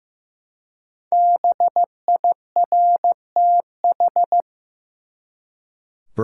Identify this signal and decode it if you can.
Morse code, heard as a keyed tone, about 15 wpm, 700 Hz. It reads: BIRTH
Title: Five Letter Words - 15wpm